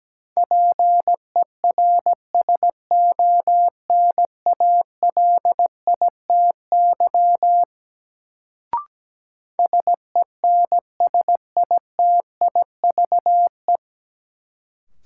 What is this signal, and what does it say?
Morse code, heard as a keyed tone, about 17 wpm, 700 Hz. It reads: PERSONALITY SENSITIVE
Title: Sentences from Top 1400 Words - New Word Drill - 17wpm